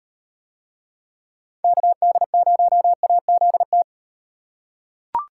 Morse code, heard as a keyed tone, about 37 wpm, 700 Hz. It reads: KD0AZT